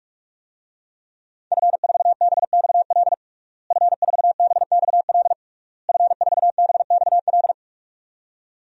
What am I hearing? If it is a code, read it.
Morse code, heard as a keyed tone, about 46 wpm, 700 Hz. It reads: F4BXL F4BXL F4BXL
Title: DXCC-100 Call Signs - 5 Characters - Repeated 3x - 45wpm